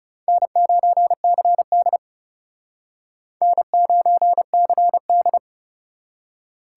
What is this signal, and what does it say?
Morse code, heard as a keyed tone, about 40 wpm, 700 Hz. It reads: N9CD N9CD